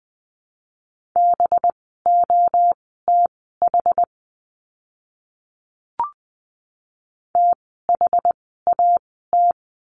Morse code, heard as a keyed tone, about 20 wpm, spaced as for 15 wpm, 700 Hz. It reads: BOTH THAT
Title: 185 Lesson 30f - Letter b - Words 20wpm